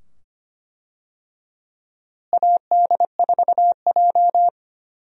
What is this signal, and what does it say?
Morse code, heard as a keyed tone, about 25 wpm, 700 Hz. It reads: AD4J
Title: USA Call Signs - 4 Characters - Repeated 2x - 25wpm